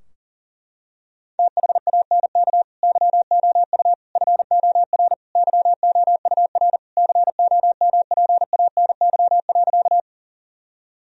Morse code, heard as a keyed tone, about 40 wpm, 700 Hz. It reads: THANK YOU FOR YOUR COMPANY.
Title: Sentences from Top 1000 Words - Review Part 3 - 40wpm